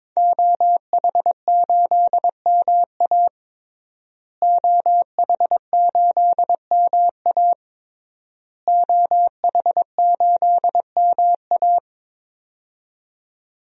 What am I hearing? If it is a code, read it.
Morse code, heard as a keyed tone, about 22 wpm, 700 Hz. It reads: OH8MA OH8MA OH8MA
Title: DXCC-100 Call Signs - 5 Characters - Repeated 3x - 22wpm